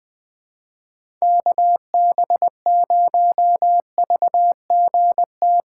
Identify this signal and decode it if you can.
Morse code, heard as a keyed tone, about 21 wpm, 700 Hz. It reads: KB0VGT
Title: USA Call Signs - 6 Characters - 4x Character Spacing - 20wpm